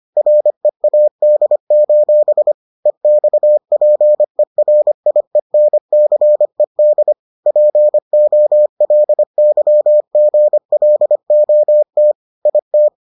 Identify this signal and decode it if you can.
Morse code, heard as a keyed tone, about 25 wpm, 600 Hz. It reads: READ: EXPERIENCED POLYGLOT IT